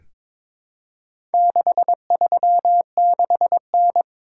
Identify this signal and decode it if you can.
Morse code, heard as a keyed tone, about 22 wpm, 700 Hz. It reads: 636N
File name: practice / 22wpm / Four Letter-Number Combinations - 22wpm